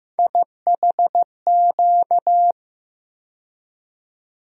Morse code, heard as a keyed tone, about 15 wpm, 700 Hz. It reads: IHQ
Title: USA Call Sign Postfixes - 3 Letters 15wpm